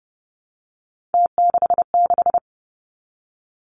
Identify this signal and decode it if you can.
Morse code, heard as a keyed tone, about 30 wpm, 700 Hz. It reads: T66
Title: Three Challenging Letter-Number Combinations 30wpm